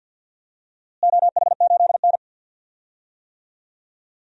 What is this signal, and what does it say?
Morse code, heard as a keyed tone, about 51 wpm, 700 Hz. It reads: OH8N
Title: DXCC-100 Call Signs - 4 Characters - Repeated 3x - 50wpm